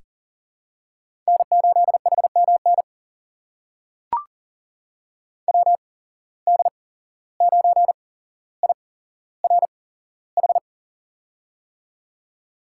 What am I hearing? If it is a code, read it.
Morse code, heard as a keyed tone, about 39 wpm, spaced as for 16 wpm, 700 Hz. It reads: N8HMN WD9IRH
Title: USA Call Signs - 8x Character Spacing - 40wpm